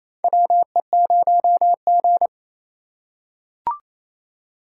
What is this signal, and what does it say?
Morse code, heard as a keyed tone, about 27 wpm, 700 Hz. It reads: WE0G
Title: USA Call Signs - 4 Characters - 4x Character Spacing - 28wpm